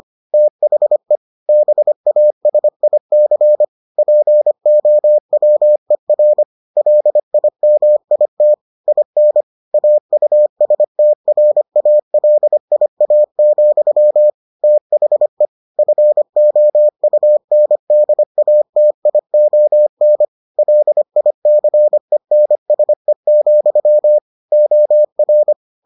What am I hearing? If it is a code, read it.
Morse code, heard as a keyed tone, about 25 wpm, 600 Hz. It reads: THE BASIC POWER LIMIT IN AUSTRALIA, THE FOUNDATION LICENSE, OR